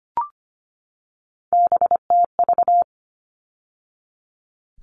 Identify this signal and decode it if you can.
Morse code, heard as a keyed tone, about 25 wpm, 700 Hz. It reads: BTV
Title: All Cut Numbers - 3-Digit 25wpm